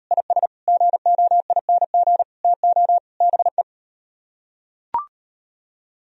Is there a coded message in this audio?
Yes: Morse code from 0.1 to 3.6 s, heard as a keyed tone, about 38 wpm, 700 Hz. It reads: IS GOING TO BE